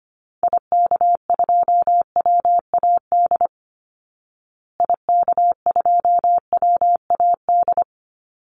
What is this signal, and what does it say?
Morse code, heard as a keyed tone, about 25 wpm, 700 Hz. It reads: IK2WAD IK2WAD
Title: Call Signs Repeated 3x - 25wpm